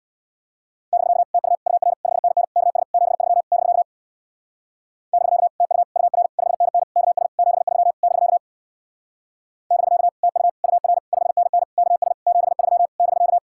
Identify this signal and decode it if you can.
Morse code, heard as a keyed tone, about 63 wpm, 700 Hz. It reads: <BK> TU UR 5NN KS 73 <BK> <BK> TU UR 5NN KS 73 <BK> <BK> TU UR 5NN KS 73 <BK>